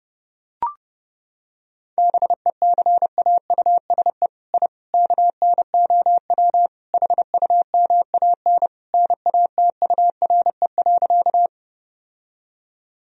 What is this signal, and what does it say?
Morse code, heard as a keyed tone, about 32 wpm, 700 Hz. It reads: BECAUSE I KNOW HUMAN NATURE.
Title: Sentences from Top 1000 Words - Review Part 3 - 30wpm